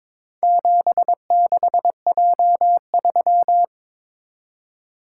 Morse code, heard as a keyed tone, about 22 wpm, 700 Hz. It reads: 76J3
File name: practice / 22wpm / Four Letter-Number Combinations - 22wpm